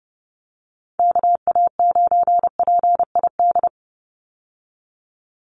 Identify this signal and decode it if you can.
Morse code, heard as a keyed tone, about 30 wpm, 700 Hz. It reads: KA9PID
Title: USA Call Signs 30wpm